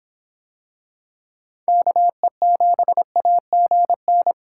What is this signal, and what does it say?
Morse code, heard as a keyed tone, about 23 wpm, 700 Hz. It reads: KE7AGN